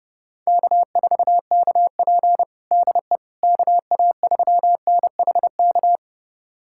Morse code, heard as a keyed tone, about 30 wpm, 700 Hz. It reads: K4KP DE KA3NHK
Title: USA Callsign DE Callsign 30wpm